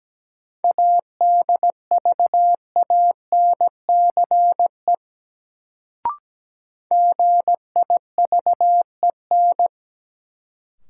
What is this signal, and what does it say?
Morse code, heard as a keyed tone, about 17 wpm, 700 Hz. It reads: ADVANCE GIVEN
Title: Sentences from Top 1400 Words - New Word Drill - 17wpm